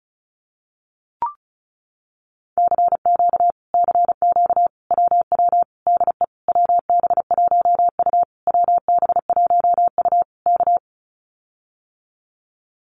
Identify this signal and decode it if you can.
Morse code, heard as a keyed tone, about 34 wpm, 700 Hz. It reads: CQ CQ WW DE WB1U WB1U K
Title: Common QSO Phrases Encore 35wpm